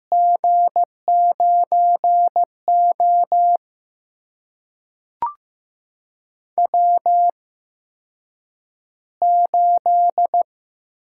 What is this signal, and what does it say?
Morse code, heard as a keyed tone, about 15 wpm, spaced as for 8 wpm, 700 Hz. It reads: G9O W8
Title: USA Call Signs - 4 Characters - 8x Character Spacing - 15wpm